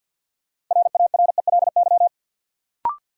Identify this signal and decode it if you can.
Morse code, heard as a keyed tone, about 53 wpm, 700 Hz. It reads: RARELY
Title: Sentences from Top 1400 Words - New Word Drill Alt - 50wpm